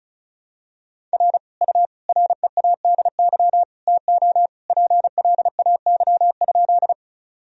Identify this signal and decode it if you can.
Morse code, heard as a keyed tone, about 36 wpm, 700 Hz. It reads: R U READY TO PLAY?
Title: Sentences from Top 400 Words and Abbreviations 35wpm